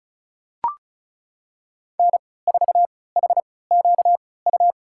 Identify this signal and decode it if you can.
Morse code, heard as a keyed tone, about 34 wpm, spaced as for 20 wpm, 700 Hz. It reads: N4HQU
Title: USA Call Signs - 5 Characters - 3x Character Spacing - 35wpm